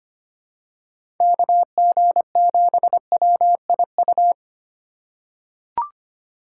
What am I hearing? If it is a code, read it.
Morse code, heard as a keyed tone, about 25 wpm, 700 Hz. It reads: KG7WIU
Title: USA Call Signs - 6 Characters - 4x Character Spacing - 25wpm